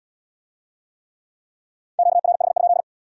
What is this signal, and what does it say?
Morse code, heard as a keyed tone, about 70 wpm, 700 Hz. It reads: QRH?